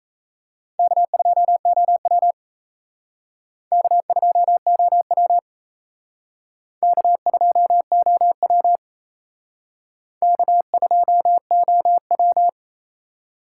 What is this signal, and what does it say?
Morse code, heard as a keyed tone, about 34 wpm, 700 Hz. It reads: K2OW K2OW K2OW K2OW
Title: Speed Racing - USA 1x2 Call Signs 28wpm